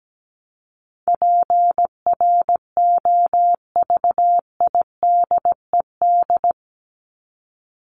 Morse code, heard as a keyed tone, about 17 wpm, 700 Hz. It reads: PROVIDED
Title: Top 1751-2000 Words 17wpm